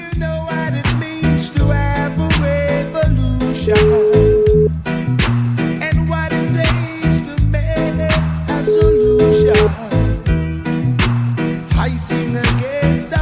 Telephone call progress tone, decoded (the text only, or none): ringback tone